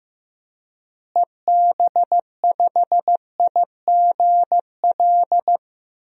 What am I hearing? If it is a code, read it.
Morse code, heard as a keyed tone, about 16 wpm, 700 Hz. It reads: EB5IGL
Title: DXCC-100 Call Signs - 6 Characters - Repeated 3x - 15wpm